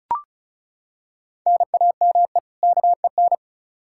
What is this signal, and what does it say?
Morse code, heard as a keyed tone, about 34 wpm, 700 Hz. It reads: NAME KEN
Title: Practice QSO - Exchange 1 - Beginning - 35wpm